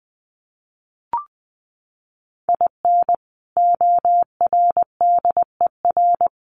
Morse code, heard as a keyed tone, about 20 wpm, 700 Hz. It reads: IN ORDER
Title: Sets of 2 Words 20wpm